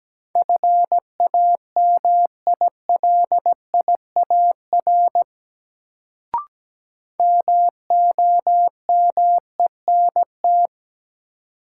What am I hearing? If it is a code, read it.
Morse code, heard as a keyed tone, about 17 wpm, 700 Hz. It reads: FAMILIAR MOMENT
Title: Sentences from Top 1000 Words - New Word Drill - 17wpm